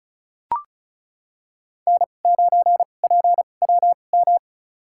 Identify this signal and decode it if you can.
Morse code, heard as a keyed tone, about 36 wpm, spaced as for 25 wpm, 700 Hz. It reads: N9PWM
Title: USA Call Signs - 5 Characters - 2x Character Spacing - 35wpm